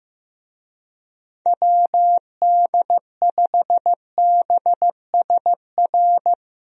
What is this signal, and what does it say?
Morse code, heard as a keyed tone, about 15 wpm, 700 Hz. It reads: WD5BSR
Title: USA Call Signs - 6 Characters - Repeated 3x - 15wpm